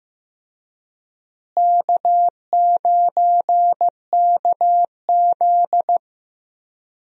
Morse code, heard as a keyed tone, about 15 wpm, 700 Hz. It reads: K9KZ
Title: USA Call Signs - 4 Characters - Repeated 3x - 15wpm